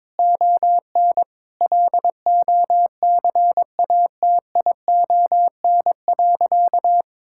Morse code, heard as a keyed tone, about 22 wpm, 700 Hz. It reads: ON LOCATION.
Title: Sentences from Top 1000 Words - Review Part 6 - 22wpm